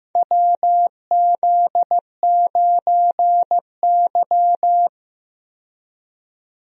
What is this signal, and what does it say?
Morse code, heard as a keyed tone, about 15 wpm, 700 Hz. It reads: WZ9Y
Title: USA Call Signs - 4 Characters - 15wpm